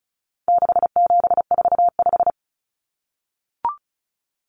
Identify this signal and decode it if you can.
Morse code, heard as a keyed tone, about 34 wpm, 700 Hz. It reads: 6745
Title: Common Cut Numbers - 4-Digit 35wpm